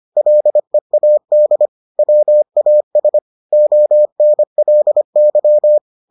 Morse code, heard as a keyed tone, about 25 wpm, 600 Hz. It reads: LEAD WAS ONLY